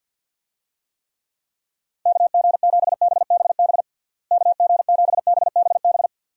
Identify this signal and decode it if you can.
Morse code, heard as a keyed tone, about 50 wpm, 700 Hz. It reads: KG7BBB KG7BBB
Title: USA Call Signs - 6 Characters - Repeated 2x - 50wpm